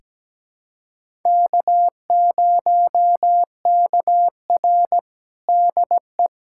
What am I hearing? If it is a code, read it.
Morse code, heard as a keyed tone, about 17 wpm, 700 Hz. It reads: K0KR DE
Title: USA Callsign DE Callsign - 4 Characters - 17wpm